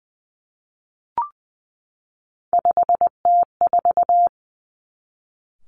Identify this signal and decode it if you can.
Morse code, heard as a keyed tone, about 20 wpm, 700 Hz. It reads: 5T4
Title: Three Challenging Letter-Number Combinations 20wpm